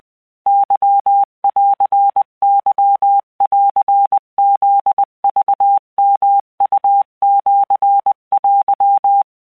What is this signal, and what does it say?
Morse code (Japanese Wabun code), heard as a keyed tone, about 20 wpm, 800 Hz. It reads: ケンケンフクヨウシテ